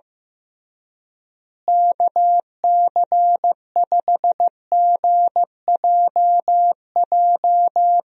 Morse code, heard as a keyed tone, about 15 wpm, 700 Hz. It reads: KC5GJJ
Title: USA Call Signs - 6 Characters - Repeated 2x - 15wpm